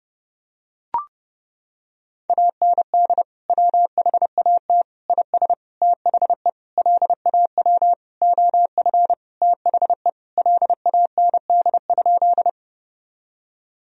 Morse code, heard as a keyed tone, about 31 wpm, 700 Hz. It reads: AND WHAT IS THE LAW OF THE LAND?
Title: Sentences from Top 1000 Words - Review Part 3 - 30wpm